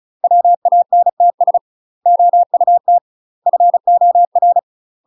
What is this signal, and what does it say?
Morse code, heard as a keyed tone, about 36 wpm, 700 Hz. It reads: WANTS OUT FOR
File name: UK MASTER QSOs TRAINING / 35WPM / UK MASTER QSOs 0007